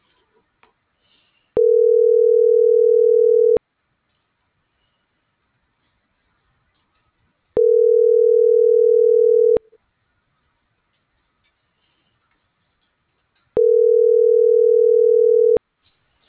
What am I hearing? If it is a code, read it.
Ringback tone (North America), 440 + 480 Hz.